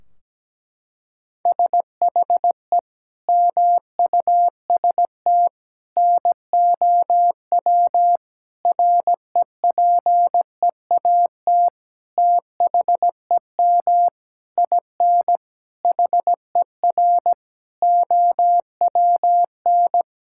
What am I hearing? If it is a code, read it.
Morse code, heard as a keyed tone, about 17 wpm, 700 Hz. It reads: SHE MUST NOW REPEAT THEM IN HER OWN
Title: Sentences from Top 1600 Words 17wpm